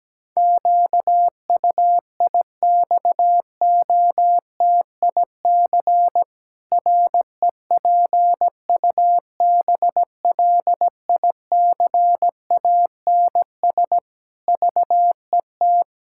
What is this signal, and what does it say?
Morse code, heard as a keyed tone, about 17 wpm, 700 Hz. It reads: QUIXOTIC REPUBLICANS VET
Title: Pangram Challenge v2 - 17wpm